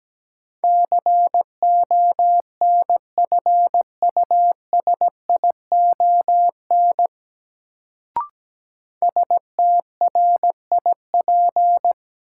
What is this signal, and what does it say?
Morse code, heard as a keyed tone, about 17 wpm, 700 Hz. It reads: CONFUSION STRIP
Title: Sentences from Top 1600 Words - New Word Drill Alt - 17wpm